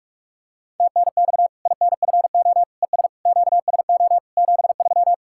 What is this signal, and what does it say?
Morse code, heard as a keyed tone, about 43 wpm, 700 Hz. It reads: TNX INFO ES QSO 73
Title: Practice QSO - Exchange 4 - Ending - 45wpm